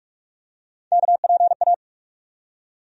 Morse code, heard as a keyed tone, about 46 wpm, 700 Hz. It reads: KPA
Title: USA Call Sign Postfixes - 3 Letters 45wpm